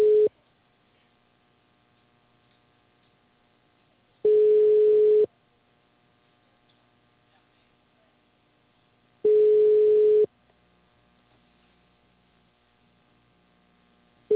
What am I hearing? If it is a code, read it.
Ringback tone (Europe, 425 Hz).